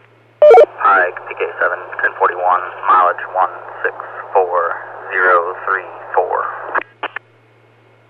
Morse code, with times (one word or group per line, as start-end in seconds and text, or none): none